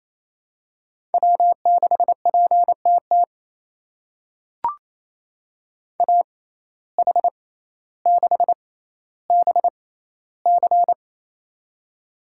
Morse code, heard as a keyed tone, about 28 wpm, spaced as for 14 wpm, 700 Hz. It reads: W6PTT AH6BC
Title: USA Call Signs - 5 Characters - 6x Character Spacing - 28wpm